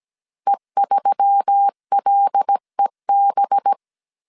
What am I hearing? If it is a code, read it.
Morse code, heard as a keyed tone, about 17 wpm, 800 Hz. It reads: E3LEB
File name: 16wpm